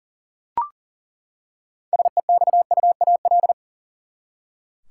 Morse code, heard as a keyed tone, about 37 wpm, 700 Hz. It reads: SEXUAL
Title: Six Letter Words - 40wpm